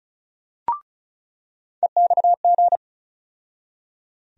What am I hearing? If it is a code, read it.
Morse code, heard as a keyed tone, about 34 wpm, 700 Hz. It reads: EXG
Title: USA Call Sign Postfixes - 3 Letters 35wpm